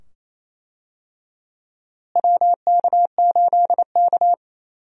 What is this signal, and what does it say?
Morse code, heard as a keyed tone, about 28 wpm, 700 Hz. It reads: WK8K
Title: USA Call Signs - 4 Characters - Repeated 2x - 28wpm